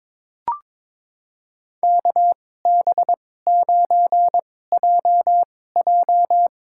Morse code, heard as a keyed tone, about 22 wpm, spaced as for 16 wpm, 700 Hz. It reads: KB9JJ
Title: USA Call Signs - 2x Character Spacing - 22wpm